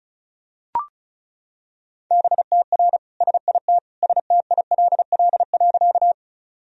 Morse code, heard as a keyed tone, about 36 wpm, 700 Hz. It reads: BTR SIT STILL.
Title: Sentences from Top 500 Words and Abbreviations 35wpm